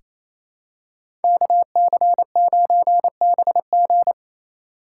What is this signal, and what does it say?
Morse code, heard as a keyed tone, about 28 wpm, 700 Hz. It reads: KC9BG
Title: K1USN SST QSO Practice - 28wpm